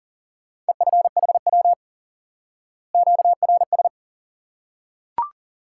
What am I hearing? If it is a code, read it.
Morse code, heard as a keyed tone, about 40 wpm, 700 Hz. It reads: EFHW QRS
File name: IQE Course 3 - Lesson 89 - YAGI - Two Elements - 40wpm